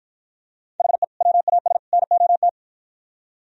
Morse code, heard as a keyed tone, about 50 wpm, 700 Hz. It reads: HE WAS NOT